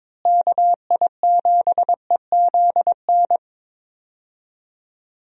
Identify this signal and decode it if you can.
Morse code, heard as a keyed tone, about 22 wpm, 700 Hz. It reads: KI7EZN
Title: USA Call Signs - 6 Characters - Repeated 2x - 22wpm